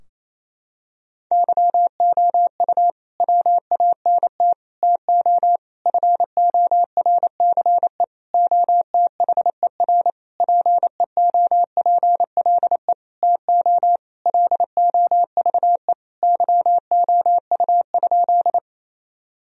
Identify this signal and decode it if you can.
Morse code, heard as a keyed tone, about 28 wpm, 700 Hz. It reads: YOU WANT TO FORCE OTHER PEOPLE TO LOVE YOU?
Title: Sentences from Top 1000 Words - Review Part 3 - 28wpm